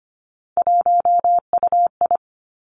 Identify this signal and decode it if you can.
Morse code, heard as a keyed tone, about 25 wpm, 700 Hz. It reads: 1UI
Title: Three Challenging Letter-Number Combinations 25wpm